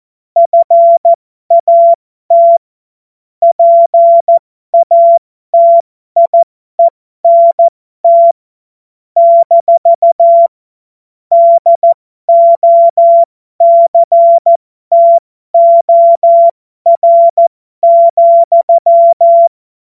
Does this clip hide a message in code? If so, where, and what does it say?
Yes: Morse code from 0.4 to 19.5 s, heard as a keyed tone, about 14 wpm, spaced as for 12 wpm, 680 Hz. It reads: FAT PATIENT - DOCTOR,